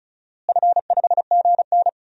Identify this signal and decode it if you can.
Morse code, heard as a keyed tone, about 36 wpm, 700 Hz. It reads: F5GN
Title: DXCC-100 Call Signs - 4 Characters - Repeated 3x - 35wpm